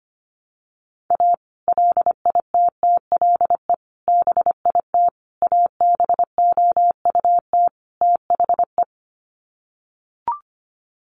Morse code, heard as a keyed tone, about 25 wpm, 700 Hz. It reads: A LITTLE BIT ABOUT THE
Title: Sets of 5 Words Encore2 25wpm